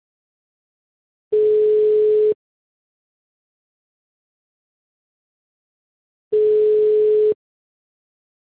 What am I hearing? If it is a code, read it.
Ringback tone (Europe, 425 Hz).